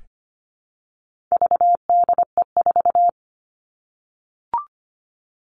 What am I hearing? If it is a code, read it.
Morse code, heard as a keyed tone, about 25 wpm, 700 Hz. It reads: VDE4
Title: All Cut Numbers - 4-Digit 25wpm